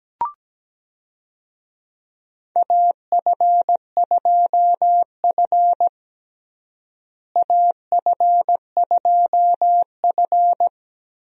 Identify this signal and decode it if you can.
Morse code, heard as a keyed tone, about 17 wpm, 700 Hz. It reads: AF2F AF2F
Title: USA Call Signs - 4 Characters - Repeated 3x - 17wpm